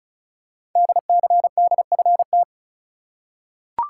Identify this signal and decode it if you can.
Morse code, heard as a keyed tone, about 36 wpm, 700 Hz. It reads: DCDFT
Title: Code Groups - 5 Letters - 35wpm